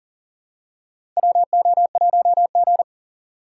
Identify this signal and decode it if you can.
Morse code, heard as a keyed tone, about 40 wpm, 700 Hz. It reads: WO1G
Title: USA Call Signs - 4 Characters - 4x Character Spacing - 40wpm